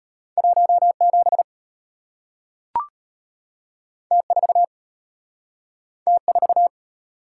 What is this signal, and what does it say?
Morse code, heard as a keyed tone, about 33 wpm, 700 Hz. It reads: 17 T4 T4